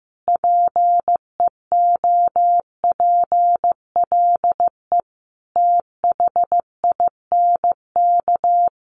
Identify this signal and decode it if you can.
Morse code, heard as a keyed tone, about 15 wpm, 700 Hz. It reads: PEOPLE THINK
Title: Sets of 5 Words Encore 15wpm